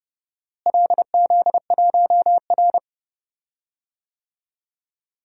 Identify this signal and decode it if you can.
Morse code, heard as a keyed tone, about 30 wpm, 700 Hz. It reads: LZ1R
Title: DXCC-100 Call Signs - 4 Characters - Repeated 3x - 30wpm